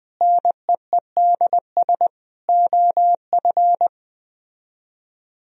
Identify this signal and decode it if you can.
Morse code, heard as a keyed tone, about 20 wpm, 700 Hz. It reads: NEEDS OF